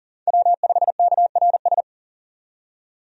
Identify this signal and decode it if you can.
Morse code, heard as a keyed tone, about 40 wpm, 700 Hz. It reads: W5KRS
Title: USA Call Signs - 5 Characters - 40wpm